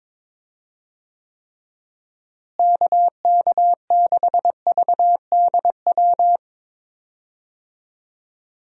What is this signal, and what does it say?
Morse code, heard as a keyed tone, about 22 wpm, 700 Hz. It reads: KK6VDW
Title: USA Call Signs - 6 Characters - Repeated 3x - 22wpm